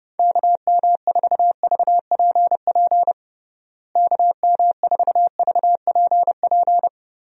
Morse code, heard as a keyed tone, about 30 wpm, 700 Hz. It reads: KM4VPP KM4VPP
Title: USA Call Signs - 6 Characters - Repeated 3x - 30wpm